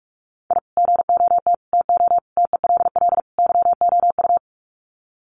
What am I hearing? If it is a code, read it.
Morse code, heard as a keyed tone, about 44 wpm, 700 Hz. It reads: I GOT TO TELL YOU
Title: Sets of 5 Words Encore2 45wpm